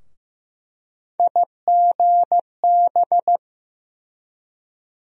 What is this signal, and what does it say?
Morse code, heard as a keyed tone, about 15 wpm, 700 Hz. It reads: IGB
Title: USA Call Sign Postfixes - 3 Letters 15wpm